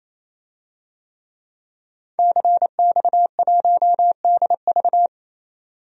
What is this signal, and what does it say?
Morse code, heard as a keyed tone, about 28 wpm, 700 Hz. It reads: CX1DV
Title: DXCC-100 Call Signs - 5 Characters - 28wpm